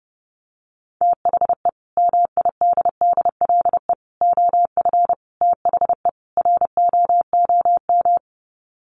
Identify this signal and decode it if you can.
Morse code, heard as a keyed tone, about 30 wpm, 700 Hz. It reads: THE MIDDLE OF THE ROOM
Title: Sets of 5 Words Encore2 30wpm